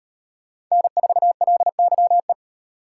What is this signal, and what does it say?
Morse code, heard as a keyed tone, about 38 wpm, 700 Hz. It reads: N4LYE